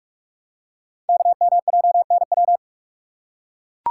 Morse code, heard as a keyed tone, about 44 wpm, 700 Hz. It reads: KMJNW
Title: Code Groups - 5 Letters - 45wpm